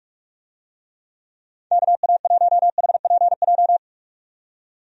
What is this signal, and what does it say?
Morse code, heard as a keyed tone, about 46 wpm, 700 Hz. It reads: KA1HPJ
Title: USA Call Signs - 6 Characters - 45wpm